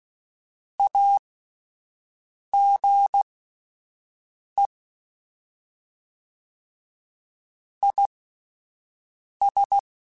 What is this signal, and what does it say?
Morse code, heard as a keyed tone, about 16 wpm, spaced as for 5 wpm, 800 Hz. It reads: AGE IS